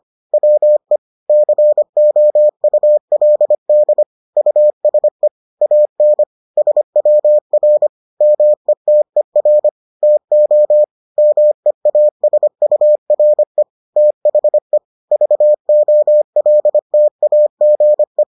Morse code, heard as a keyed tone, about 25 wpm, 600 Hz. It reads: WE COULD USE AN SWR METER TO MEASURE THE VOLTAGE